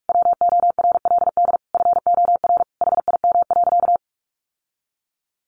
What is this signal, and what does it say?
Morse code, heard as a keyed tone, about 44 wpm, 700 Hz. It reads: WORLD FOR HIM.
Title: Sentences from Top 1300 Words 45wpm